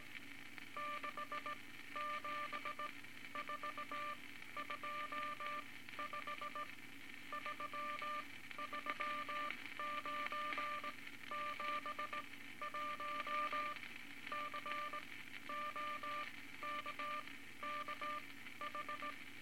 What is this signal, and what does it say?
Morse code, heard as a keyed tone, about 18 wpm, spaced as for 13 wpm, 1190 Hz, with noise. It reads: 67425 33971 COKKH
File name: Morse Code Records / 18 A LCM 164 Ten Words Per Minute